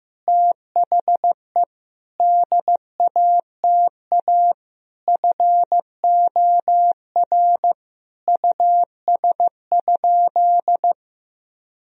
Morse code, heard as a keyed tone, about 15 wpm, 700 Hz. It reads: THE DATA FOR US?
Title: Sentences from Top 1000 Words - Review Part 3 - 15wpm